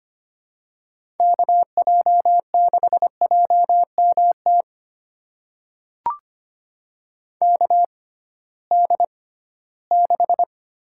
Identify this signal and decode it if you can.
Morse code, heard as a keyed tone, about 25 wpm, spaced as for 16 wpm, 700 Hz. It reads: KJ6JMT KD6
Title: USA Call Signs - 6 Characters - 6x Character Spacing - 25wpm